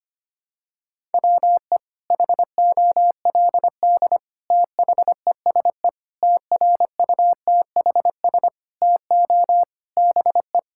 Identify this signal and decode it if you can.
Morse code, heard as a keyed tone, about 25 wpm, 700 Hz. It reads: WE HOLD THESE TRUTHS TO BE
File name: practice / 25wpm / Sentences with 9-Word Phrases - 25wpm